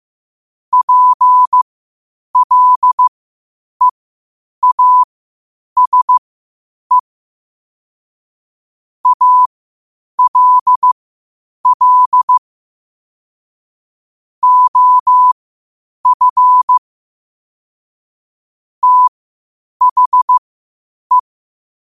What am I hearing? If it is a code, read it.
Morse code, heard as a keyed tone, about 15 wpm, spaced as for 8 wpm, 1000 Hz. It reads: PLEASE ALL OF THE
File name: qotdfiles / 8wpm